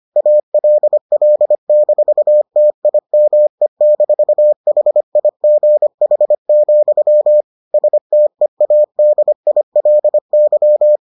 Morse code, heard as a keyed tone, about 25 wpm, 600 Hz. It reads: ALL-TIME-HIGH, STEADILY